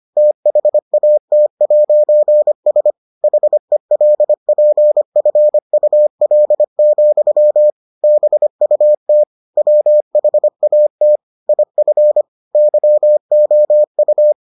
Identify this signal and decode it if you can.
Morse code, heard as a keyed tone, about 25 wpm, 600 Hz. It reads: THAT'S HELPFUL, BUT WHAT IF YOU